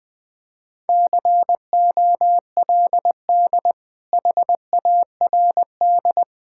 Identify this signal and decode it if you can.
Morse code, heard as a keyed tone, about 21 wpm, 700 Hz. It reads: COLD HARD
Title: Sentences from Top 1000 Words - Review Part 5 - 20wpm